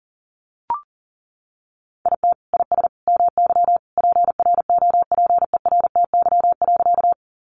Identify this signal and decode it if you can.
Morse code, heard as a keyed tone, about 40 wpm, 700 Hz. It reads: IT IS MY PROPERTY.
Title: Sentences from Top 1000 Words 40wpm